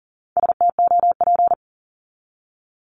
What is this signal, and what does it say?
Morse code, heard as a keyed tone, about 40 wpm, 700 Hz. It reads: STOP
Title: Top 500 Words 40wpm